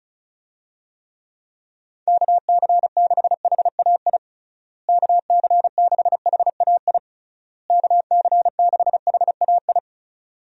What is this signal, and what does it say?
Morse code, heard as a keyed tone, about 36 wpm, 700 Hz. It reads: KC6HAI KC6HAI KC6HAI
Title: USA Call Signs - 6 Characters - Repeated 3x - 35wpm